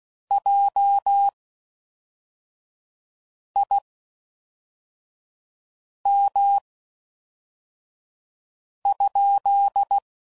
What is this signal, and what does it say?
Morse code, heard as a keyed tone, about 16 wpm, spaced as for 4 wpm, 800 Hz. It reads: JIM?